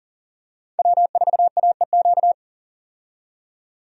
Morse code, heard as a keyed tone, about 40 wpm, 700 Hz. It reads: W4AEQ
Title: USA Call Signs - 5 Characters - 40wpm